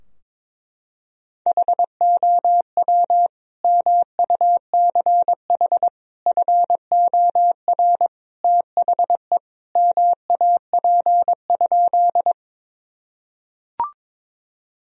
Morse code, heard as a keyed tone, about 22 wpm, 700 Hz. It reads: HOW MUCH FOR THE MAP?
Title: Sentences from Top 1000 Words - Review Part 2 - 22wpm